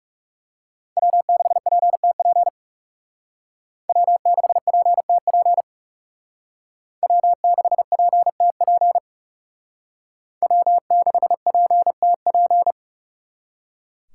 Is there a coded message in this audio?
Yes: Morse code from 1.0 to 12.7 s, heard as a keyed tone, about 39 wpm, 700 Hz. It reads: W6PTP W6PTP W6PTP W6PTP